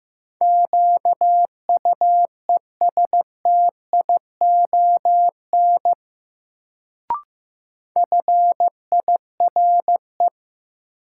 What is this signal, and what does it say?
Morse code, heard as a keyed tone, about 15 wpm, 700 Hz. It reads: QUESTION FIRE
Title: Sentences from Top 400 Words - New Word Drill - 15wpm